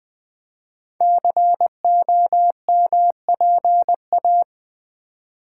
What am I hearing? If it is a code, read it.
Morse code, heard as a keyed tone, about 20 wpm, 700 Hz. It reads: COMPA
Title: Build A Word - 5 Letters At A Time - 10-Letter Words - 20wpm